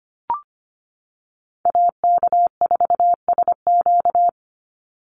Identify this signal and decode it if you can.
Morse code, heard as a keyed tone, about 25 wpm, 700 Hz. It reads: AK4SQ
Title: Call Signs - 25wpm